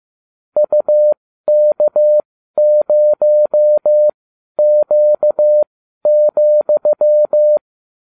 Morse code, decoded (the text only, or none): UK0Q,